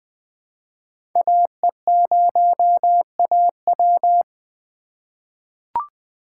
Morse code, heard as a keyed tone, about 20 wpm, 700 Hz. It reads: AE0AW
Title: USA Call Signs - 5 Characters - 4x Character Spacing - 20wpm